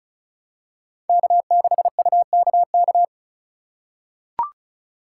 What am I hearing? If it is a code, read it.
Morse code, heard as a keyed tone, about 36 wpm, 700 Hz. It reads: K6UKK
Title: USA Call Signs - 5 Characters - 4x Character Spacing - 35wpm